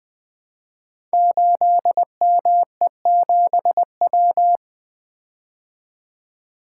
Morse code, heard as a keyed tone, about 20 wpm, 700 Hz. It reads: 8ME7W
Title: Code Groups - 5 Letter:Numbers - 20wpm